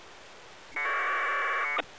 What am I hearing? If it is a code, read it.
Packet radio (AX.25, as in APRS).